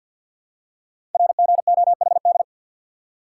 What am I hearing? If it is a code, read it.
Morse code, heard as a keyed tone, about 50 wpm, 700 Hz. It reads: RGOHD
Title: Code Groups - 5 Letters - 50wpm